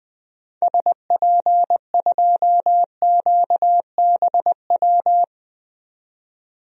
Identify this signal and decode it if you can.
Morse code, heard as a keyed tone, about 21 wpm, 700 Hz. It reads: SP2QBW
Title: DXCC-100 Call Signs - 6 Characters - 20wpm